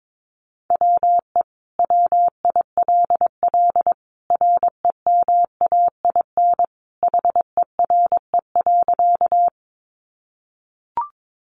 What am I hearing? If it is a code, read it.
Morse code, heard as a keyed tone, about 22 wpm, 700 Hz. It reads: WE WILL REMAIN HERE.
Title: Sentences from Top 1200 Words 22wpm